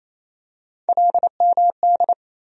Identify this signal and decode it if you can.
Morse code, heard as a keyed tone, about 28 wpm, 700 Hz. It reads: LMD
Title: USA Call Sign Postfixes - 3 Letters 28wpm